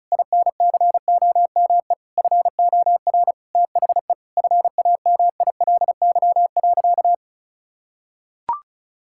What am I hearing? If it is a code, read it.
Morse code, heard as a keyed tone, about 34 wpm, 700 Hz. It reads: INCOME FOR THE FAMILY.
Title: Sentences from Top 1000 Words - Review Part 5 - 35wpm